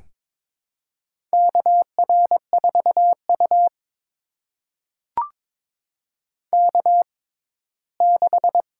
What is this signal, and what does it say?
Morse code, heard as a keyed tone, about 22 wpm, spaced as for 15 wpm, 700 Hz. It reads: KR4U K6
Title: USA Call Signs - 4 Characters - 6x Character Spacing - 22wpm